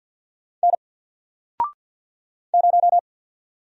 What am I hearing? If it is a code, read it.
Morse code, heard as a keyed tone, about 48 wpm, 700 Hz. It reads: N 0
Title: Single Letter-Number Alt 50wpm